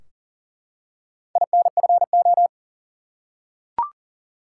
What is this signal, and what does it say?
Morse code, heard as a keyed tone, about 39 wpm, 700 Hz. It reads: INFO
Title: Top 180 QSO Elements 40wpm